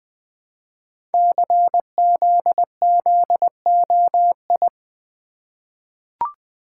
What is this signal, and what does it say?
Morse code, heard as a keyed tone, about 20 wpm, 700 Hz. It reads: CZZOI
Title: Code Groups - 5 Letters - 20wpm